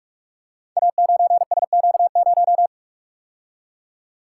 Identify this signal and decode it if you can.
Morse code, heard as a keyed tone, about 44 wpm, 700 Hz. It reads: A9SQ0
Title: Code Groups - 5 Letter:Numbers - 45wpm